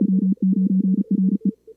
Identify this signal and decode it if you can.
Morse code, heard as a keyed tone, about 34 wpm, 190 Hz. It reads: W9RE
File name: CWSprint / Feb2022 / k5tr-feb-na-sprint-cw-k5ot -2300